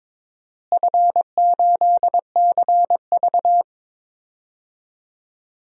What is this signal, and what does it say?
Morse code, heard as a keyed tone, about 22 wpm, 700 Hz. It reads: F8CV
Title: DXCC-100 Call Signs - 4 Characters - Repeated 3x - 22wpm